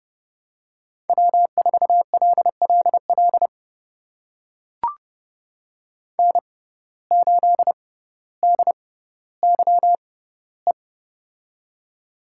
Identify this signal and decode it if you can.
Morse code, heard as a keyed tone, about 30 wpm, spaced as for 15 wpm, 700 Hz. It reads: W4LLL N8DYE
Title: USA Call Signs - 5 Characters - 6x Character Spacing - 30wpm